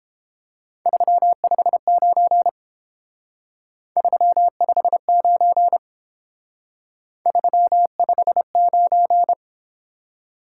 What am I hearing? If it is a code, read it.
Morse code, heard as a keyed tone, about 30 wpm, 700 Hz. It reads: 359 359 359